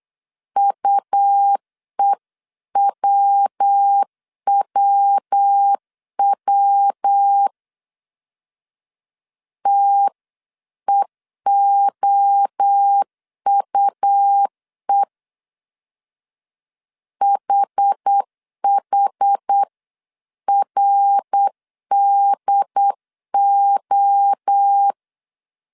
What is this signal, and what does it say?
Morse code, heard as a keyed tone, about 8 wpm, 800 Hz. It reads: UEWWW TEOUE HHRDO